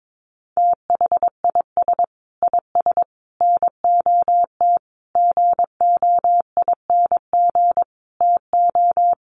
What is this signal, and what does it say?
Morse code, heard as a keyed tone, about 22 wpm, 700 Hz. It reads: THIS IS NOT GOING TO
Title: Sets of 5 Words Encore2 22wpm